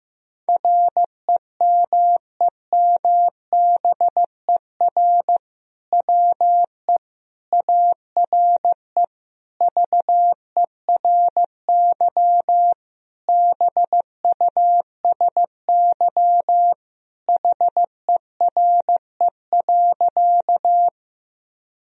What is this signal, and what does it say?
Morse code, heard as a keyed tone, about 15 wpm, 700 Hz. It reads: REMEMBER WE ARE VERY BUSY HERE.
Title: Sentences from Top 1000 Words - Review Part 3 - 15wpm